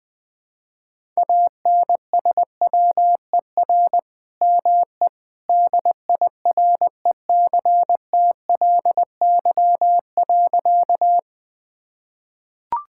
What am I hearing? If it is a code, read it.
Morse code, heard as a keyed tone, about 20 wpm, 700 Hz. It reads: ANSWER ME DIRECTLY.
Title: Sentences from Top 1000 Words - Review Part 5 - 20wpm